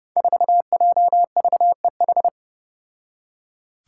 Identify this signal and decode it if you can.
Morse code, heard as a keyed tone, about 31 wpm, 700 Hz. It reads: 4JVEH
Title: Five Challenging Letter-Number Combinations 30wpm